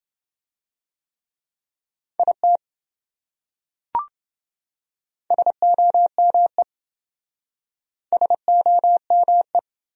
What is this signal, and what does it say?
Morse code, heard as a keyed tone, about 27 wpm, 700 Hz. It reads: IT SOME SOME